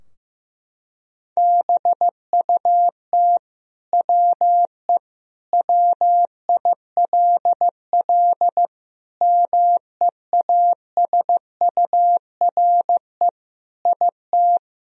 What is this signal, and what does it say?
Morse code, heard as a keyed tone, about 15 wpm, 700 Hz. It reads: BUT WE WILL MEASURE IT
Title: Sentences from Top 1000 Words - Review Part 2 - 15wpm